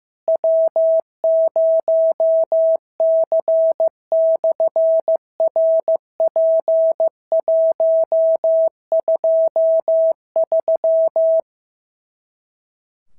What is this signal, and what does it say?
Morse code, heard as a keyed tone, about 15 wpm, 650 Hz. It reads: W0C/RP123
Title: SOTA QSO Practice 15wpm